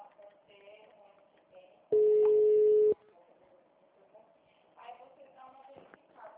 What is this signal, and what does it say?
Ringback tone (Europe, 425 Hz), with speech.